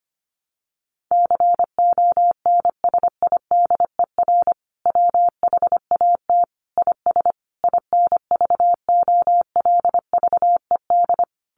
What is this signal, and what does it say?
Morse code, heard as a keyed tone, about 25 wpm, 700 Hz. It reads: CONSIDER WHAT IS INVOLVED
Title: Sentences from Top 800 Words 25wpm